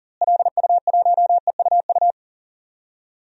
Morse code, heard as a keyed tone, about 40 wpm, 700 Hz. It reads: LU1EUU
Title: DXCC-100 Call Signs - 6 Characters - Repeated 2x - 40wpm